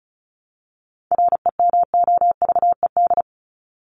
Morse code, heard as a keyed tone, about 34 wpm, 700 Hz. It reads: REMOVED
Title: ED Suffix - Top 100 Words with Base Words 35wpm